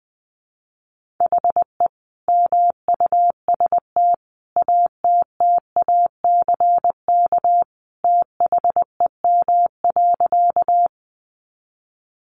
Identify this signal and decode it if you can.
Morse code, heard as a keyed tone, about 20 wpm, 700 Hz. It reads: HE MUST ATTACK THEM.
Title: Sentences from Top 1500 Words 20wpm